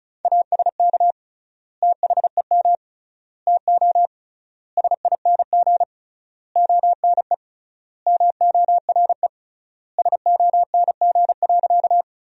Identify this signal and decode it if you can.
Morse code, heard as a keyed tone, about 34 wpm, 700 Hz. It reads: ASK THEM TO SING ONE MORE SONG.
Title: 3x Word-Spacing - Sentences from Top 400 Words - 35wpm